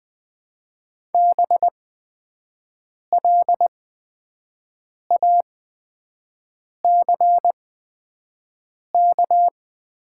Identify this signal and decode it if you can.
Morse code, heard as a keyed tone, about 20 wpm, spaced as for 5 wpm, 700 Hz. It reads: BLACK
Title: 8x Character Spacing - Top 500 Words - 20wpm